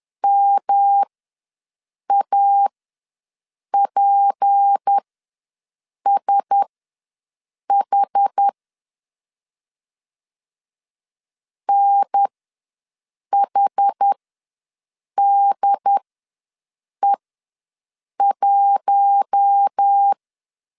Morse code, heard as a keyed tone, about 10 wpm, spaced as for 6 wpm, 800 Hz. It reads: MAPSH NHDE1
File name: LEZIONI / 7 GF9Z / 7 GF ws 10wpm 3918